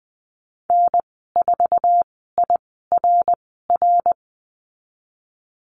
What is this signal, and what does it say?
Morse code, heard as a keyed tone, about 20 wpm, spaced as for 15 wpm, 700 Hz. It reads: N4IRR
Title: 055 Lesson 10g - Letter r - Call Signs 20wpm